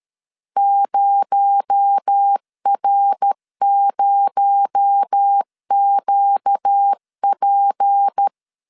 Morse code, heard as a keyed tone, about 13 wpm, 800 Hz. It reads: 0R0QP
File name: LEZIONI / 6 PKX0 / 6 PKX0 12wpm 2231